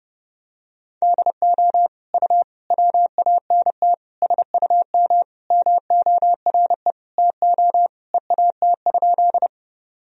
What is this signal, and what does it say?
Morse code, heard as a keyed tone, about 32 wpm, 700 Hz. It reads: DO U WANT SUM MORE TO EAT?
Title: Sentences from Top 300 Words and Abbreviations 30wpm